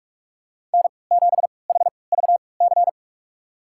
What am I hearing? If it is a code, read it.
Morse code, heard as a keyed tone, about 46 wpm, spaced as for 26 wpm, 700 Hz. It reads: N7HVC
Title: USA Call Signs - 3x Character Spacing - 45wpm